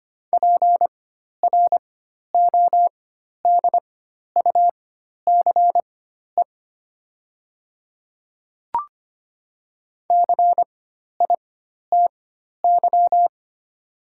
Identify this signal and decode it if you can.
Morse code, heard as a keyed tone, about 25 wpm, spaced as for 12 wpm, 690 Hz. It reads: PRODUCE CITY
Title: 4x Character Spacing - Top 500 Words - 25wpm